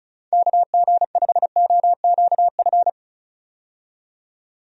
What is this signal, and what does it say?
Morse code, heard as a keyed tone, about 36 wpm, 700 Hz. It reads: KG5OQF
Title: USA Call Signs - 6 Characters - 35wpm